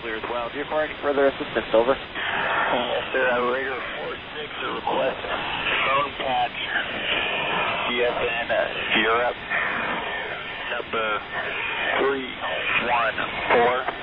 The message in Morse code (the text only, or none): none